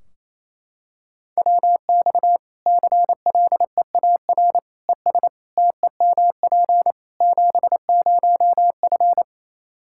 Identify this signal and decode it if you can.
Morse code, heard as a keyed tone, about 28 wpm, 700 Hz. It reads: WX CLEAR ES TEMP 70F
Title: Practice QSO - Exchange 2 - Info - 28wpm